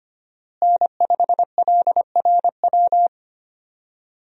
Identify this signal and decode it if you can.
Morse code, heard as a keyed tone, about 25 wpm, 700 Hz. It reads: N5LRW
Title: USA Call Signs - 5 Characters - Repeated 2x - 25wpm